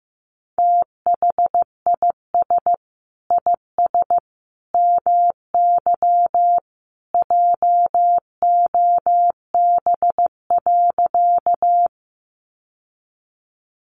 Morse code, heard as a keyed tone, about 15 wpm, 700 Hz. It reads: THIS IS MY JOB.
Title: Sentences from Top 600 Words 15wpm